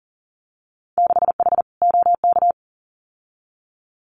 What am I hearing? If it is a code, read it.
Morse code, heard as a keyed tone, about 40 wpm, 700 Hz. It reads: 6H OK
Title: Winter Field Day Exchanges Encore 40wpm